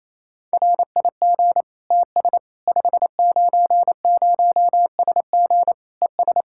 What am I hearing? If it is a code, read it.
Morse code, heard as a keyed tone, about 28 wpm, 700 Hz. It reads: RIG TS 590SG ES
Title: Practice QSO - Exchange 2 - Info - 28wpm